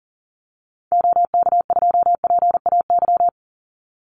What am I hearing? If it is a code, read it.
Morse code, heard as a keyed tone, about 40 wpm, 700 Hz. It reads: OK2PAY
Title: Call Signs - 40wpm